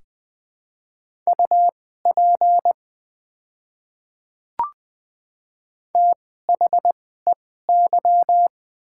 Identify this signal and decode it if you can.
Morse code, heard as a keyed tone, about 21 wpm, spaced as for 15 wpm, 700 Hz. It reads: UP THEY
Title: 2x Character Spacing - Top 100 Words - 20wpm